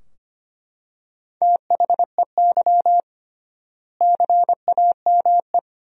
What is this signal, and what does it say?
Morse code, heard as a keyed tone, about 25 wpm, 700 Hz. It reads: THEY CAME
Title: 3x Word-Spacing - Sentences from Top 300 Words - 25wpm